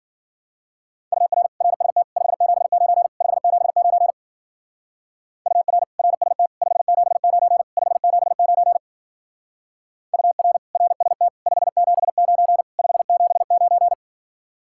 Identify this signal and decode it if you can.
Morse code, heard as a keyed tone, about 49 wpm, 700 Hz. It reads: UR RST 579 579 UR RST 579 579 UR RST 579 579
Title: Speed-Racing Common QSO Phrases - 40wpm